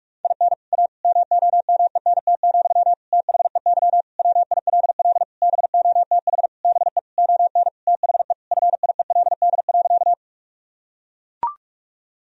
Morse code, heard as a keyed tone, about 41 wpm, 700 Hz. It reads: IN A MOMENT, THEY WILL BOTH BE ON THE FIELD.
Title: Sentences from Top 1000 Words - Review Part 2 - 45wpm